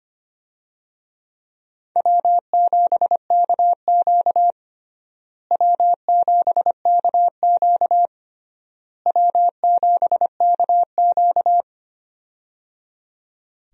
Morse code, heard as a keyed tone, about 25 wpm, 700 Hz. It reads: W7KQ W7KQ W7KQ
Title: USA Call Signs - 4 Characters - Repeated 3x - 25wpm